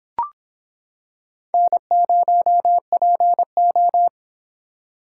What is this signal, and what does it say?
Morse code, heard as a keyed tone, about 26 wpm, 700 Hz. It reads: N0PO